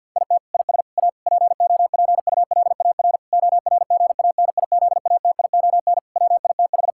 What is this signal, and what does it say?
Morse code, heard as a keyed tone, about 50 wpm, 700 Hz. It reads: IT IS A POPULAR ORGANIZATION WITH
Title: Sentences from Top 1000 Words - Review Part 5 - 50wpm